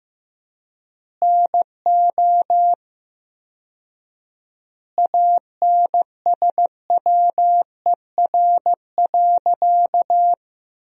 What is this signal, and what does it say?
Morse code, heard as a keyed tone, about 15 wpm, 700 Hz. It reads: NO ANSWER.
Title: 4x Word-Spacing - Sentences from Top 500 Words - 15wpm